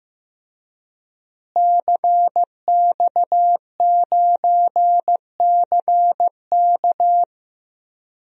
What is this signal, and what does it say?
Morse code, heard as a keyed tone, about 15 wpm, 700 Hz. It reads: CX9CK
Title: DXCC-100 Call Signs - 5 Characters - Repeated 2x - 15wpm